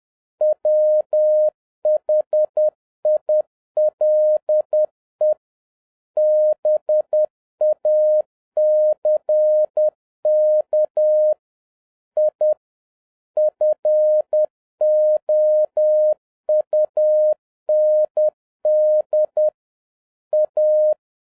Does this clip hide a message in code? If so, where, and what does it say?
Yes: Morse code from 0.4 to 20.9 s, heard as a keyed tone, about 10 wpm, 610 Hz. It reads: WHILE BACK I FOUND A